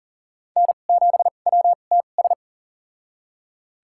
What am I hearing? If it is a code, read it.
Morse code, heard as a keyed tone, about 40 wpm, spaced as for 29 wpm, 700 Hz. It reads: N7WTS
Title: USA Call Signs - 5 Characters - 2x Character Spacing - 40wpm